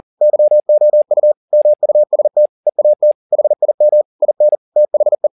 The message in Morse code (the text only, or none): YOU MUST EAT HIM IN THE